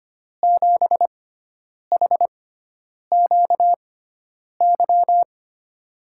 Morse code, heard as a keyed tone, about 25 wpm, spaced as for 9 wpm, 700 Hz. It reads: 7HQY
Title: USA Call Signs - 6 Characters - 6x Character Spacing - 25wpm